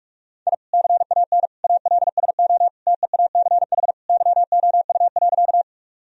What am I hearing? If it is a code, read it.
Morse code, heard as a keyed tone, about 46 wpm, 700 Hz. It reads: I CAN ALSO TEACH YOU.
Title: Sentences from Top 1000 Words - Review Part 2 - 45wpm